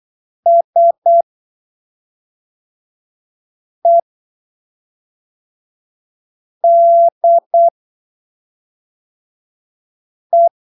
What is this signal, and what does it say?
Morse code, heard as a keyed tone, about 8 wpm, spaced as for 3 wpm, 680 Hz. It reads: SEDE
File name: arquivos / texto / texto 08 08ppm